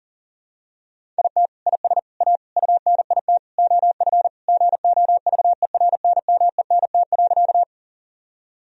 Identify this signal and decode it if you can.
Morse code, heard as a keyed tone, about 40 wpm, 700 Hz. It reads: IT IS A UNIT OF GOVERNMENT.
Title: Sentences from Top 1000 Words - Review Part 2 - 40wpm